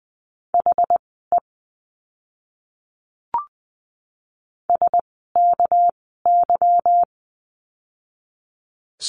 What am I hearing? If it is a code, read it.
Morse code, heard as a keyed tone, about 20 wpm, spaced as for 14 wpm, 700 Hz. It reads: HE SKY
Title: 199 Lesson 32f - Letter k - Words 20wpm